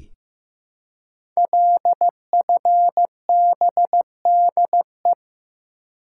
Morse code, heard as a keyed tone, about 15 wpm, 700 Hz. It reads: LFBDE
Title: Code Groups - 5 Letters - 15wpm